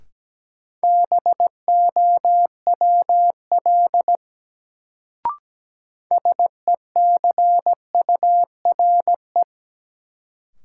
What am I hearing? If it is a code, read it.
Morse code, heard as a keyed tone, about 17 wpm, 700 Hz. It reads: BOWL SECURE
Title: Sentences from Top 1200 Words - New Word Drill Alt - 17wpm